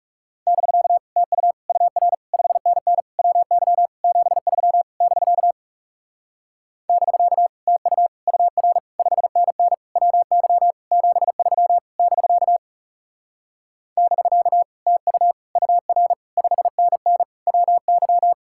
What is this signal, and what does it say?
Morse code, heard as a keyed tone, about 37 wpm, 700 Hz. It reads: <BK> TU UR 5NN WY 73 <BK> <BK> TU UR 5NN WY 73 <BK> <BK> TU UR 5NN WY
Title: Speed-Racing Common QSO Phrases - 30wpm